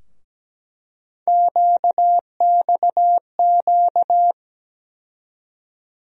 Morse code, heard as a keyed tone, about 17 wpm, 700 Hz. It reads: QXQ
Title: USA Call Sign Postfixes - 3 Letters 17wpm